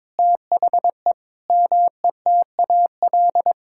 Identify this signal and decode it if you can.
Morse code, heard as a keyed tone, about 22 wpm, 700 Hz. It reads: THE METAL
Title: Sentences from Top 1000 Words - Review Part 6 - 22wpm